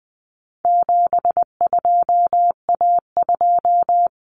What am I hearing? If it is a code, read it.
Morse code, heard as a keyed tone, about 20 wpm, 700 Hz. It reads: 72A2
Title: Common Cut Numbers - 4-Digit 20wpm